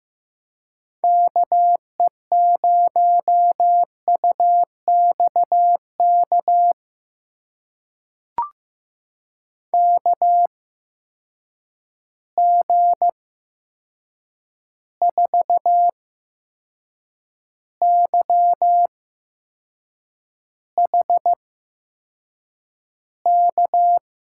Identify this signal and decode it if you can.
Morse code, heard as a keyed tone, about 15 wpm, spaced as for 6 wpm, 700 Hz. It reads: KE0UXK KG4YHK
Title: USA Call Signs - 6 Characters - 8x Character Spacing - 15wpm